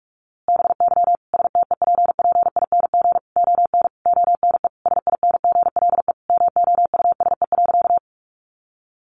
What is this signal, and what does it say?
Morse code, heard as a keyed tone, about 44 wpm, 700 Hz. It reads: BY STEPPING ON ONE SINGLE MOUSE.
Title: A Sound Of Thunder 45wpm